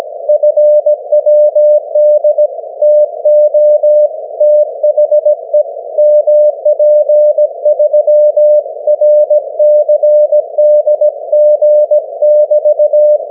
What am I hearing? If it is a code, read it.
Morse code, heard as a keyed tone, about 17 wpm, 600 Hz. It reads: FWD TO THE MP3 RCDG<BT>
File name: GW3UEP / 16Dec12 9AM G3XIZ 501kHz